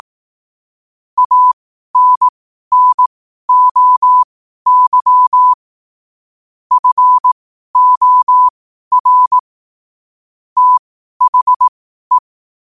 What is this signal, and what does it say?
Morse code, heard as a keyed tone, about 18 wpm, spaced as for 12 wpm, 1000 Hz. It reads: ANNOY FOR THE
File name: qotdfiles / 12wpm